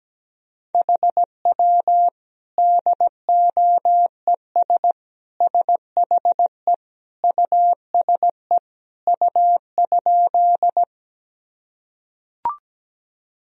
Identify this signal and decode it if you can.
Morse code, heard as a keyed tone, about 17 wpm, 700 Hz. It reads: HW DOES SHE USE U?
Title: Sentences from Top 200 Words and Abbreviations 17wpm